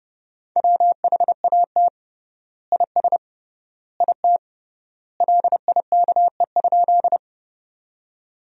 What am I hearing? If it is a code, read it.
Morse code, heard as a keyed tone, about 32 wpm, 700 Hz. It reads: WHAT IS IT LIKE?
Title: 3x Word-Spacing - Sentences from Top 100 Words - 30wpm